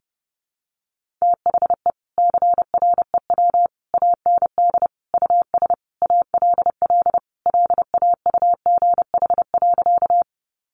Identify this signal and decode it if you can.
Morse code, heard as a keyed tone, about 30 wpm, 700 Hz. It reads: THE CREW AND US ALL LAUGH.
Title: Sentences from Top 1500 Words 30wpm